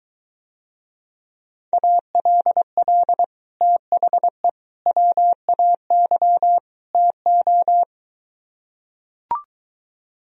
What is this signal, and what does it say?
Morse code, heard as a keyed tone, about 23 wpm, 700 Hz. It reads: ALL THE WAY TO